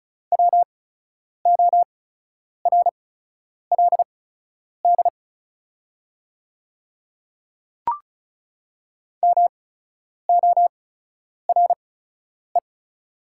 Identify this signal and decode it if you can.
Morse code, heard as a keyed tone, about 36 wpm, spaced as for 10 wpm, 690 Hz. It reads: WORLD MORE
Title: 8x Character Spacing - Top 200 Words - 35wpm